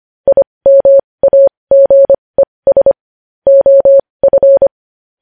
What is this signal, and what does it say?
Morse code, heard as a keyed tone, about 25 wpm, spaced as for 20 wpm, 550 Hz. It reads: IMAGES OF
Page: news headlines in morse code at 20 wpm